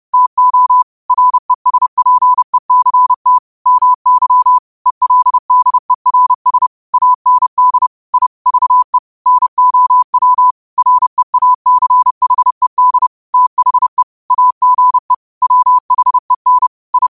Morse code, heard as a keyed tone, about 30 wpm, 1000 Hz. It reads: TO RESPECT MY ELDERS AND I VE NOW REACHED THE AGE WHEN I